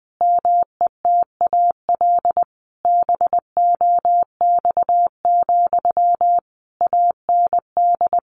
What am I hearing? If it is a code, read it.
Morse code, heard as a keyed tone, about 20 wpm, 700 Hz. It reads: METAL BOX, AND
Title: A Sound Of Thunder 20wpm